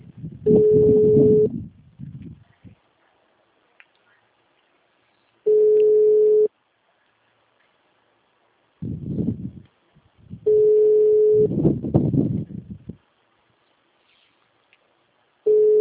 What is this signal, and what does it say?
Ringback tone (Europe, 425 Hz).